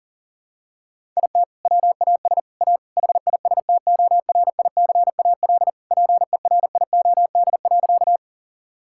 Morse code, heard as a keyed tone, about 37 wpm, 700 Hz. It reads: IT WAS A HISTORICAL PERIOD.
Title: Sentences from Top 1000 Words - Review Part 5 - 40wpm